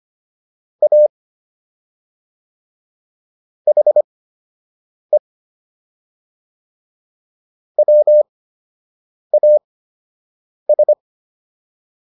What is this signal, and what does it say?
Morse code, heard as a keyed tone, about 25 wpm, spaced as for 7 wpm, 600 Hz. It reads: A HE WAS